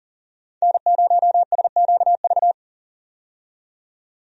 Morse code, heard as a keyed tone, about 40 wpm, 700 Hz. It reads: N0SQV
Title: USA Call Signs - 5 Characters - 40wpm